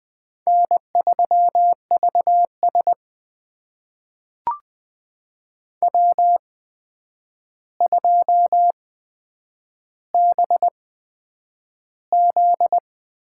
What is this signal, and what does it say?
Morse code, heard as a keyed tone, about 20 wpm, spaced as for 9 wpm, 700 Hz. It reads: N3VS W2BZ
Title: USA Call Signs - 4 Characters - 8x Character Spacing - 20wpm